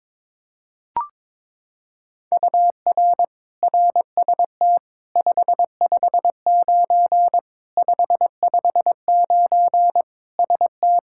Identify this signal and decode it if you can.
Morse code, heard as a keyed tone, about 22 wpm, 700 Hz. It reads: UR RST 559 559 ST
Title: Practice QSO - Exchange 1 - Beginning - 22wpm